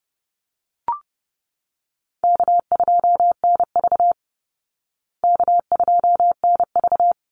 Morse code, heard as a keyed tone, about 30 wpm, 700 Hz. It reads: K2NV K2NV
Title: Easy Call Signs Repeated 2x - 30wpm